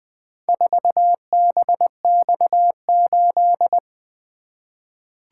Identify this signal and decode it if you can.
Morse code, heard as a keyed tone, about 20 wpm, 700 Hz. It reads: 4BX8
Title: Four Letter-Number Combinations - 20wpm